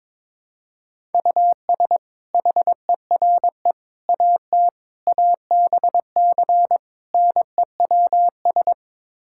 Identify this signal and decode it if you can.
Morse code, heard as a keyed tone, about 22 wpm, 700 Hz. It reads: US HERE AT ABC NEWS
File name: practice / 22wpm / Build A Phrase - 5-Word Phrases - 22wpm